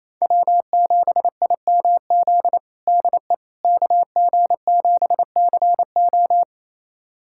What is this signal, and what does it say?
Morse code, heard as a keyed tone, about 28 wpm, 700 Hz. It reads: W7IMZ DE KG7CO
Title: USA Callsign DE Callsign - 5 Characters - 28wpm